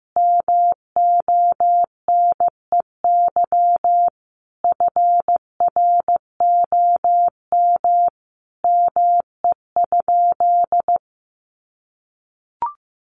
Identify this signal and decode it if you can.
Morse code, heard as a keyed tone, about 15 wpm, 700 Hz. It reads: MONEY FROM ME?
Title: Sentences from Top 1500 Words 15wpm